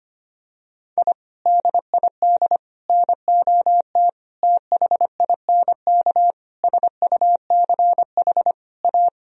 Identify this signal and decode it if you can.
Morse code, heard as a keyed tone, about 25 wpm, 700 Hz. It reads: I DID NOT THINK SUCH A
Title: Sentences from Top 1000 Words - Review Part 1 - 25wpm